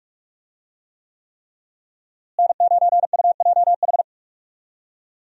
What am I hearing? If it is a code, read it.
Morse code, heard as a keyed tone, about 44 wpm, 700 Hz. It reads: N9UJH
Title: USA Call Signs - 5 Characters - Repeated 3x - 45wpm